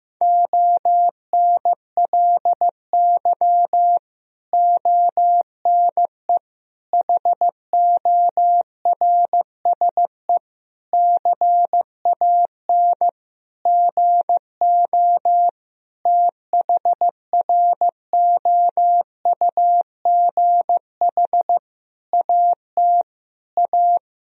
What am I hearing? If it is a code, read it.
Morse code, heard as a keyed tone, about 15 wpm, 700 Hz. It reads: ONLY ONE HORSE CAN GO THROUGH AT A
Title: Sentences from Top 1000 Words - Review Part 1 - 15wpm